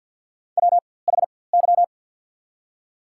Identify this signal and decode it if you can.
Morse code, heard as a keyed tone, about 53 wpm, spaced as for 24 wpm, 690 Hz. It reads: WHY
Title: 4x Character Spacing - Top 300 Words - 50wpm